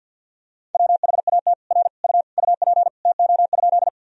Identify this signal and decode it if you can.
Morse code, heard as a keyed tone, about 52 wpm, 700 Hz. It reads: WHAT R U UP TO?
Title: Sentences from Top 100 Words and Abbreviations 50wpm